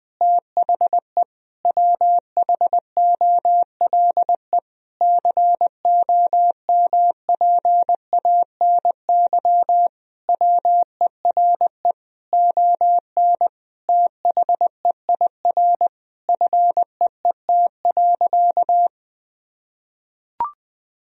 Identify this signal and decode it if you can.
Morse code, heard as a keyed tone, about 21 wpm, 700 Hz. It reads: THE WHOLE COMPANY WERE ON THEIR FEET.
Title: Sentences from Top 1000 Words - Review Part 3 - 20wpm